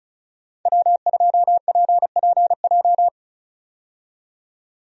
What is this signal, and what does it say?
Morse code, heard as a keyed tone, about 36 wpm, 700 Hz. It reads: W2PPJ
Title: USA Call Signs - 5 Characters - 35wpm